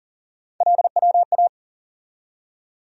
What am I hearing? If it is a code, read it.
Morse code, heard as a keyed tone, about 40 wpm, 700 Hz. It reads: LWA
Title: USA Call Sign Postfixes - 3 Letters 40wpm